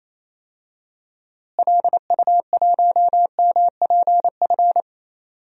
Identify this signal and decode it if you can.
Morse code, heard as a keyed tone, about 28 wpm, 700 Hz. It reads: LU1MPF
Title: DXCC-100 Call Signs - 6 Characters - Repeated 2x - 28wpm